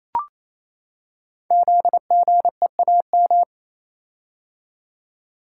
Morse code, heard as a keyed tone, about 28 wpm, 700 Hz. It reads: ZGEAM
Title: Code Groups - 5 Letters - 28wpm